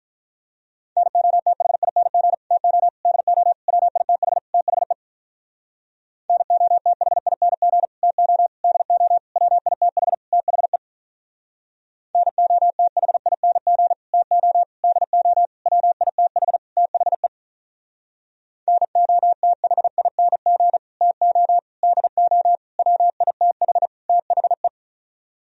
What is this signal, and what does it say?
Morse code, heard as a keyed tone, about 45 wpm, 700 Hz. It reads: NOTHING TO DO WITH THE NOTHING TO DO WITH THE NOTHING TO DO WITH THE NOTHING TO DO WITH THE